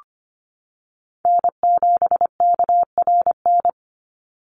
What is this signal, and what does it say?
Morse code, heard as a keyed tone, about 25 wpm, 700 Hz. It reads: N7KRN
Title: NA SOTA Chaser Callsigns 25wpm